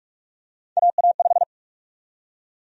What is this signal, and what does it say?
Morse code, heard as a keyed tone, about 43 wpm, 700 Hz. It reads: AA5
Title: USA Call Sign Prefixes - 45wpm